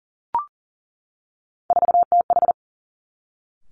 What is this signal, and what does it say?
Morse code, heard as a keyed tone, about 37 wpm, 700 Hz. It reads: 4TH
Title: Three Challenging Letter-Number Combinations 40wpm